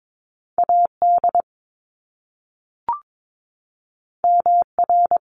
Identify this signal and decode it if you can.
Morse code, heard as a keyed tone, about 22 wpm, 700 Hz. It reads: AD MR
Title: Top Two Letter Combinations in Words 22wpm